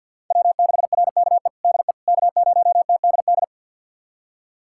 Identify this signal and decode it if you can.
Morse code, heard as a keyed tone, about 47 wpm, 700 Hz. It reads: W6RKE DE K0TDD
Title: USA Callsign DE Callsign - 5 Characters - 50wpm